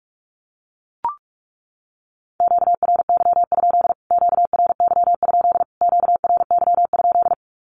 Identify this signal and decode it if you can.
Morse code, heard as a keyed tone, about 44 wpm, 700 Hz. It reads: QRY? QRY? QRY?
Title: Q-Codes Repeated 3x 45wpm